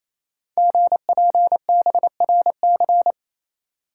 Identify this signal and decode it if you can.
Morse code, heard as a keyed tone, about 27 wpm, 700 Hz. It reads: GPBRC
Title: Code Groups - 5 Letters - 28wpm